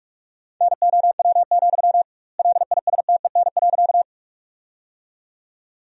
Morse code, heard as a keyed tone, about 41 wpm, 700 Hz. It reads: NOW, LISTEN.
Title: 2x Word-Spacing - Sentences from Top 400 Words - 45wpm